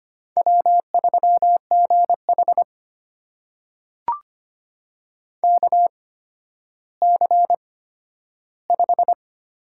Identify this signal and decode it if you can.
Morse code, heard as a keyed tone, about 25 wpm, spaced as for 12 wpm, 700 Hz. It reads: W3GH KC5
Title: USA Call Signs - 8x Character Spacing - 25wpm